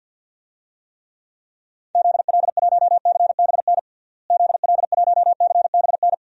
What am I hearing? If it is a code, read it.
Morse code, heard as a keyed tone, about 52 wpm, 700 Hz. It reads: ZL1CBN ZL1CBN
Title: DXCC-100 Call Signs - 6 Characters - Repeated 3x - 50wpm